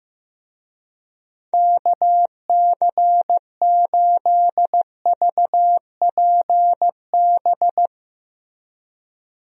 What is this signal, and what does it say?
Morse code, heard as a keyed tone, about 15 wpm, 700 Hz. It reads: KC8VPB
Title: USA Call Signs - 6 Characters - Repeated 2x - 15wpm